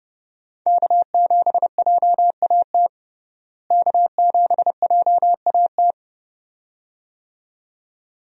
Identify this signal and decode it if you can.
Morse code, heard as a keyed tone, about 30 wpm, 700 Hz. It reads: K7JAT K7JAT
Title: USA Call Signs - 5 Characters - Repeated 2x - 30wpm